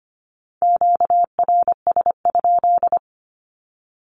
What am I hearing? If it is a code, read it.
Morse code, heard as a keyed tone, about 25 wpm, 700 Hz. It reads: QRS?
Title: Q-Codes 25wpm